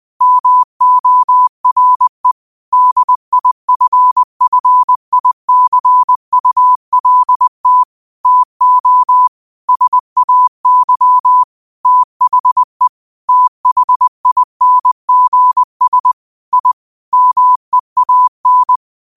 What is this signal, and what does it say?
Morse code, heard as a keyed tone, about 20 wpm, 1000 Hz. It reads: MORE DIFFICULT TO SAY THE THINGS I MEAN
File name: qotdfiles / 20wpm